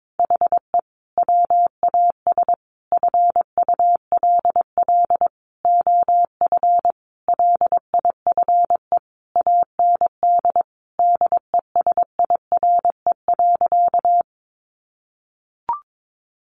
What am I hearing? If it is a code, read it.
Morse code, heard as a keyed tone, about 22 wpm, 700 Hz. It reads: HE WAS FULL OF LIFE AND DESIRE.
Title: Sentences from Top 1100 Words 22wpm